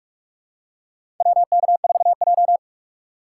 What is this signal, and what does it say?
Morse code, heard as a keyed tone, about 48 wpm, 700 Hz. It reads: WK4J
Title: USA Call Signs - 6x Character Spacing - 45wpm